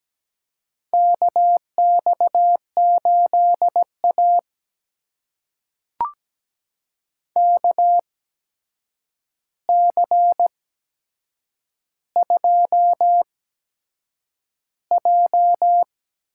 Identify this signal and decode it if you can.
Morse code, heard as a keyed tone, about 17 wpm, spaced as for 7 wpm, 700 Hz. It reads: KX8A KC2J
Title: USA Call Signs - 4 Characters - 8x Character Spacing - 17wpm